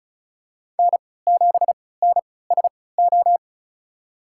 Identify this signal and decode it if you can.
Morse code, heard as a keyed tone, about 34 wpm, spaced as for 20 wpm, 700 Hz. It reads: N7NSO
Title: USA Call Signs - 5 Characters - 3x Character Spacing - 35wpm